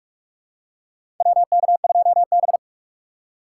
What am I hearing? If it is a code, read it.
Morse code, heard as a keyed tone, about 45 wpm, 700 Hz. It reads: WK2B
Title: USA Call Signs - 4 Characters - 2x Character Spacing - 45wpm